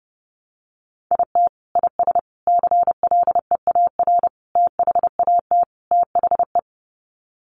Morse code, heard as a keyed tone, about 30 wpm, 700 Hz. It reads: IT IS CLEAR THAT THE
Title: Sets of 5 Words Encore 30wpm